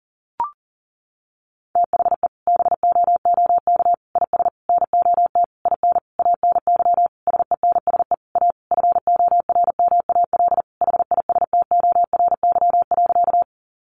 Morse code, heard as a keyed tone, about 40 wpm, 700 Hz. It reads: THE BOOK IS NOT IN ANY SENSE A FORMAL HISTORY.
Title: Sentences from Top 1300 Words 40wpm